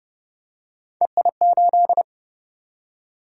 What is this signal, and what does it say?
Morse code, heard as a keyed tone, about 30 wpm, 700 Hz. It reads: EI8